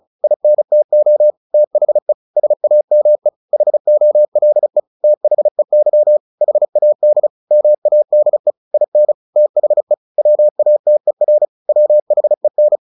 Morse code, heard as a keyed tone, about 34 wpm, 600 Hz. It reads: INTO THE SAME HOLE THEY HAD MADE IN THE WATER WHEN
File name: the old man and the sea 2 35wpm